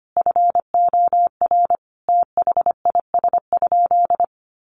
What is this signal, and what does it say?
Morse code, heard as a keyed tone, about 25 wpm, 700 Hz. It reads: FOR THIS?
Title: Sentences from Top 1400 Words 25wpm